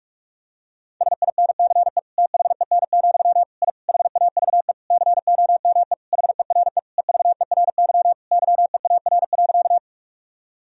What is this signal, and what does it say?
Morse code, heard as a keyed tone, about 41 wpm, 700 Hz. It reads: SINCE THEN, I HAVE COME HERE EVERY YEAR.
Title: Sentences from Top 1000 Words - Review Part 1 - 45wpm